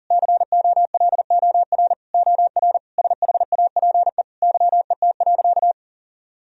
Morse code, heard as a keyed tone, about 40 wpm, 700 Hz. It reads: COLOR OR SHAPE YET.
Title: Sentences from Top 1000 Words - Review Part 3 - 40wpm